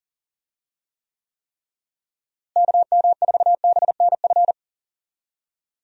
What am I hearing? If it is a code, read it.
Morse code, heard as a keyed tone, about 40 wpm, 700 Hz. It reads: KM4BNF
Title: USA Call Signs - 6 Characters - Repeated 3x - 40wpm